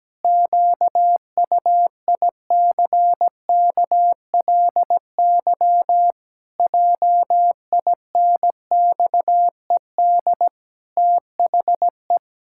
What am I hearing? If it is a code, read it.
Morse code, heard as a keyed tone, about 17 wpm, 700 Hz. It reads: QUICKLY JINXED THE
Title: Pangram Challenge v2 - 17wpm